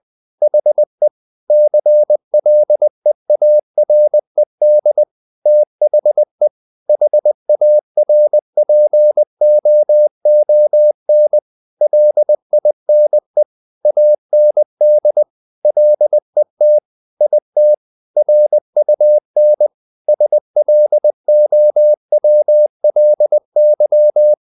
Morse code, heard as a keyed tone, about 20 wpm, 600 Hz. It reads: HE CLEARED THE HARPOON LINE AND LET IT RUN SLOWLY
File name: the old man and the sea part 2 20wpm